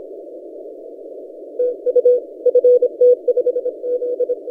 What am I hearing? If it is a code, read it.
Morse code, heard as a keyed tone, about 28 wpm, 490 Hz. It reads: TU FT5Z
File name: posts / ft5zm-30cw- -1450z